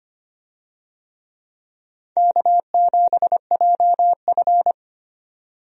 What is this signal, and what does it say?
Morse code, heard as a keyed tone, about 25 wpm, 700 Hz. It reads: K7JF
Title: USA Call Signs - 4 Characters - 25wpm